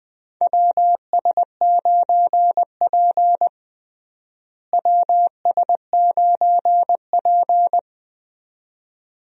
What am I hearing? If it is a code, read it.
Morse code, heard as a keyed tone, about 20 wpm, 700 Hz. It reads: WS9P WS9P
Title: USA Call Signs - 4 Characters - Repeated 2x - 20wpm